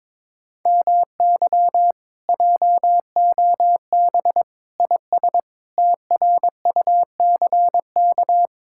Morse code, heard as a keyed tone, about 22 wpm, 700 Hz. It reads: MY JOB IS TRUCK
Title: Speed-Racing Common QSO Phrases - 22wpm